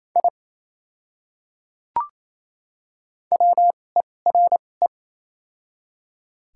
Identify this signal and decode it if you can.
Morse code, heard as a keyed tone, about 28 wpm, spaced as for 20 wpm, 700 Hz. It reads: I WERE
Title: 2x Character Spacing - Top 200 Words - 28wpm